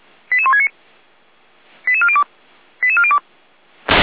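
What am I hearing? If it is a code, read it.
Five-tone selective call (ZVEI), heard as quick runs of five tones, tones of about 80 ms. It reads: 88169 88402 88402
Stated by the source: ZVEI five-tone selective call received on SDR.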